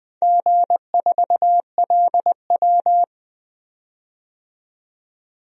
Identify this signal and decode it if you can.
Morse code, heard as a keyed tone, about 20 wpm, 700 Hz. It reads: G4LW
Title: DXCC-100 Call Signs - 4 Characters - 20wpm